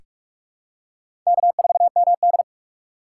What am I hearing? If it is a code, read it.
Morse code, heard as a keyed tone, about 42 wpm, 700 Hz. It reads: K4MD
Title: USA Call Signs - 4 Characters - 2x Character Spacing - 45wpm